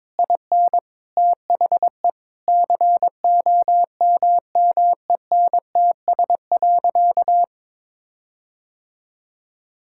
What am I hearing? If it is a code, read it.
Morse code, heard as a keyed tone, about 22 wpm, 700 Hz. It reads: IN THE COMMENTS.
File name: practice / 22wpm / Sentences with 8-Word Phrases - 22wpm